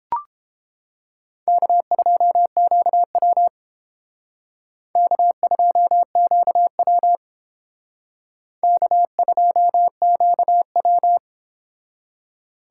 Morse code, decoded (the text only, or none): K2QW K2QW K2QW